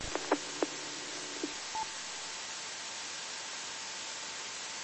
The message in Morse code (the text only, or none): none